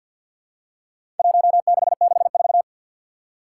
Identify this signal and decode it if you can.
Morse code, heard as a keyed tone, about 47 wpm, 700 Hz. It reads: 1664
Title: Four Challenging Letter-Number Combinations 50wpm